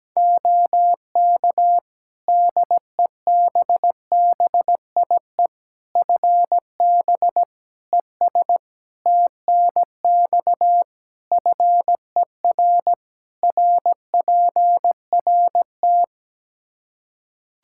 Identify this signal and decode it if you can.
Morse code, heard as a keyed tone, about 17 wpm, 700 Hz. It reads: OK DEBBIE FB ES TNX FER RPRT
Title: Practice QSO - Exchange 2 - Info - 17wpm